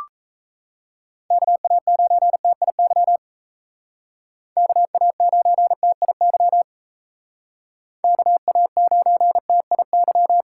Morse code, heard as a keyed tone, about 36 wpm, 700 Hz. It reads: KA9TIY KA9TIY KA9TIY